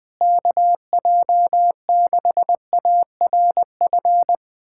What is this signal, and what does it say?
Morse code, heard as a keyed tone, about 20 wpm, 700 Hz. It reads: KJ6ARF
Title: USA Callsign DE Callsign - 6 Characters - 20wpm